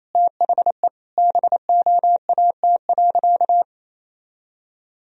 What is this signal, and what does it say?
Morse code, heard as a keyed tone, about 28 wpm, 700 Hz. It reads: THE BOAT.
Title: Sentences from Top 1000 Words - Review Part 4 - 28wpm